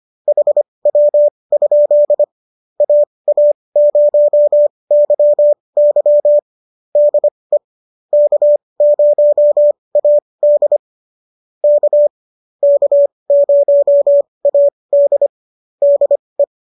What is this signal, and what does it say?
Morse code, heard as a keyed tone, about 25 wpm, spaced as for 20 wpm, 590 Hz. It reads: HW? AA0YY DE K0AD K K0AD DE